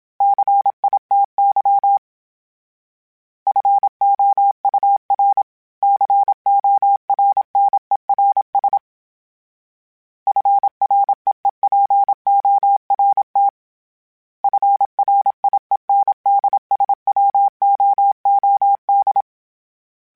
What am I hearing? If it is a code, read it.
Morse code, heard as a keyed tone, about 27 wpm, 800 Hz. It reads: CITY FOUR CORNERS FREEPORT FRIENDSWOOD
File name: fromSMs / FromSMs mp3s / Cities@25